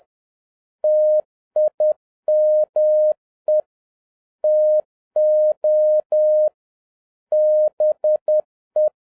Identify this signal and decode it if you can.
Morse code, heard as a keyed tone, about 10 wpm, 610 Hz. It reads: TIME TO BE
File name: 2011-07-mp3 / n56 BACK FROM A BIG ACCIDENT 10wpm